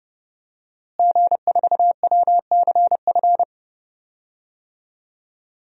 Morse code, heard as a keyed tone, about 32 wpm, 700 Hz. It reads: G4WCF
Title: DXCC-100 Call Signs - 5 Characters - Repeated 3x - 30wpm